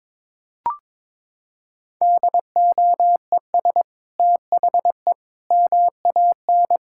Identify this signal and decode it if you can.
Morse code, heard as a keyed tone, about 22 wpm, 700 Hz. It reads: DOES THE MAN
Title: Sentences from Top 1000 Words - Review Part 5 - 22wpm